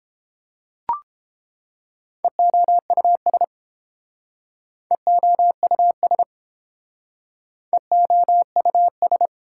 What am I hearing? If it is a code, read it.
Morse code, heard as a keyed tone, about 29 wpm, 700 Hz. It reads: EOUS EOUS EOUS